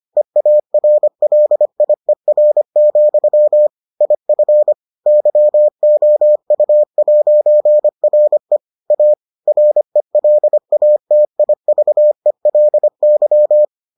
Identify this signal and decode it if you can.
Morse code, heard as a keyed tone, about 25 wpm, 600 Hz. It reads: EARLIER, IF YOU'RE A RELATIVELY